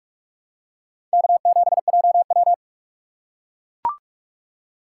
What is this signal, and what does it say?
Morse code, heard as a keyed tone, about 44 wpm, 690 Hz. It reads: K7JW
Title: USA Call Signs - 4 Characters - 4x Character Spacing - 45wpm